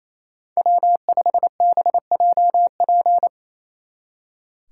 Morse code, heard as a keyed tone, about 35 wpm, 700 Hz. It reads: W5BJP